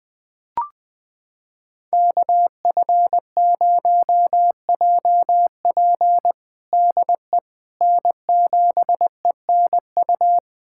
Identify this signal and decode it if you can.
Morse code, heard as a keyed tone, about 20 wpm, 700 Hz. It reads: KF0JP DE N7ENU
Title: USA Callsign DE Callsign - 5 Characters - 20wpm